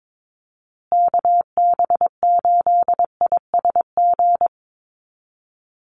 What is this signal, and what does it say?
Morse code, heard as a keyed tone, about 22 wpm, 700 Hz. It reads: KB8ISG
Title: USA Call Signs Repeated 2x 22wpm